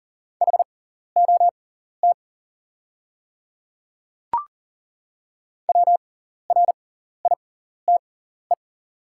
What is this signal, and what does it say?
Morse code, heard as a keyed tone, about 38 wpm, spaced as for 14 wpm, 700 Hz. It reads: HOT WRITE
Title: 6x Character Spacing - Top 100 Words - 40wpm